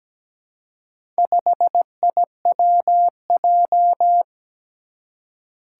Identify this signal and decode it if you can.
Morse code, heard as a keyed tone, about 17 wpm, 700 Hz. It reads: 5IWJ
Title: Four Challenging Letter-Number Combinations 17wpm